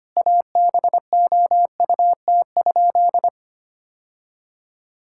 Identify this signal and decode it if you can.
Morse code, heard as a keyed tone, about 25 wpm, 700 Hz. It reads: ABOUT?
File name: practice / 25wpm / Sentences with 2-Word Phrases - 25wpm